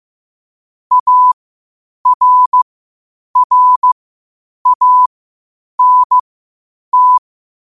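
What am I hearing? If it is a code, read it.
Morse code, heard as a keyed tone, about 15 wpm, spaced as for 8 wpm, 1000 Hz. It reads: ARRANT
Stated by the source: Morse practice at 8 WPM from Morse Resource.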